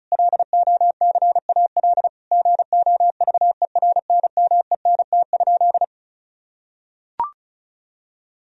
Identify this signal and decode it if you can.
Morse code, heard as a keyed tone, about 36 wpm, 700 Hz. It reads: LOCAL GOVERNMENT?
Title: Sentences from Top 1000 Words - Review Part 4 - 35wpm